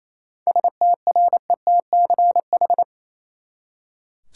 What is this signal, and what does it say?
Morse code, heard as a keyed tone, about 28 wpm, 700 Hz. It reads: STRETCH
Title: Seven Letter Words - 28wpm